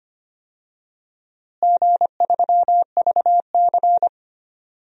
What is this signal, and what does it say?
Morse code, heard as a keyed tone, about 25 wpm, 700 Hz. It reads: G3VC
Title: DXCC-100 Call Signs - 4 Characters - Repeated 3x - 25wpm